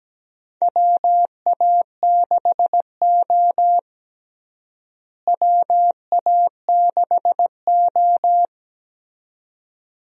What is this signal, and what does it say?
Morse code, heard as a keyed tone, about 17 wpm, 700 Hz. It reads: WA6O WA6O
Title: USA Call Signs - 4 Characters - Repeated 2x - 17wpm